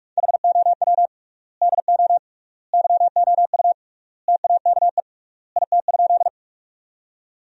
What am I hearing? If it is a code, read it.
Morse code, heard as a keyed tone, about 42 wpm, 700 Hz. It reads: HOW DO YOU TAKE IT?
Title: 3x Word-Spacing - Sentences from Top 200 Words - 45wpm